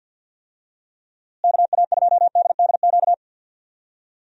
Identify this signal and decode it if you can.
Morse code, heard as a keyed tone, about 50 wpm, 700 Hz. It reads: KA2DDQ
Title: USA Call Signs - 6 Characters - Repeated 3x - 50wpm